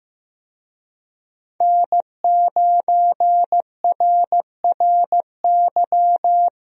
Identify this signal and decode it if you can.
Morse code, heard as a keyed tone, about 15 wpm, 700 Hz. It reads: N9RRY
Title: USA Call Signs - 5 Characters - Repeated 2x - 15wpm